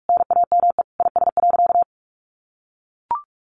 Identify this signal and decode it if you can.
Morse code, heard as a keyed tone, about 44 wpm, 700 Hz. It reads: NAME IS.
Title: Sentences from Top 1100 Words 45wpm